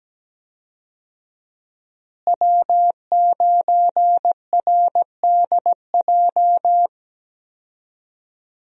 Morse code, heard as a keyed tone, about 17 wpm, 700 Hz. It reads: W9RDJ
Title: USA Call Signs - 5 Characters - Repeated 2x - 17wpm